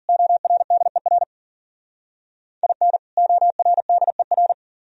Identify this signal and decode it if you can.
Morse code, heard as a keyed tone, about 50 wpm, 700 Hz. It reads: ORDER IN ORDER